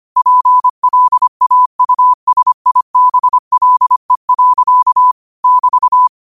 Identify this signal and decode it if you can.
Morse code, heard as a keyed tone, about 25 wpm, 1000 Hz. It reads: PLAUSIBLE. <BT>
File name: qotdfiles / 25wpm